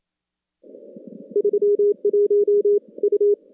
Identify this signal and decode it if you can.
Morse code, heard as a keyed tone, about 27 wpm, 410 Hz. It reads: 31 U
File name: contests / CW-SS-k5tr- -2100